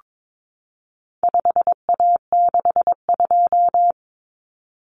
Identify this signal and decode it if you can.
Morse code, heard as a keyed tone, about 22 wpm, 700 Hz. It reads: 5A62
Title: Common Cut Numbers - 4-Digit 22wpm